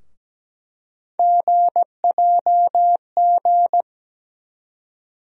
Morse code, heard as a keyed tone, about 17 wpm, 700 Hz. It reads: GJG
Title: USA Call Sign Postfixes - 3 Letters 17wpm